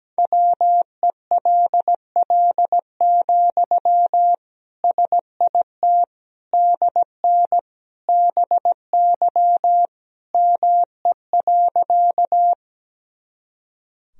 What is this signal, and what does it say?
Morse code, heard as a keyed tone, about 17 wpm, 700 Hz. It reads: WELL, SIT DN BY ME.
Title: Sentences from Top 500 Words and Abbreviations 17wpm